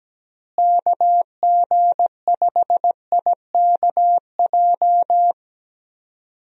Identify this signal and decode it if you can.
Morse code, heard as a keyed tone, about 17 wpm, 700 Hz. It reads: KG5IKJ
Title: USA Call Signs - 6 Characters - Repeated 2x - 17wpm